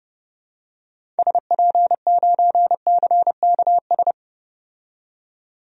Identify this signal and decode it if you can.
Morse code, heard as a keyed tone, about 30 wpm, 700 Hz. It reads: SP9CKS
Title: DXCC-100 Call Signs - 6 Characters - 30wpm